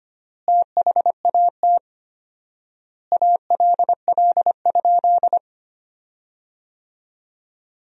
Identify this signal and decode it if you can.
Morse code, heard as a keyed tone, about 25 wpm, 700 Hz. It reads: THAT ALL?
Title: 4x Word-Spacing - Sentences from Top 200 Words - 25wpm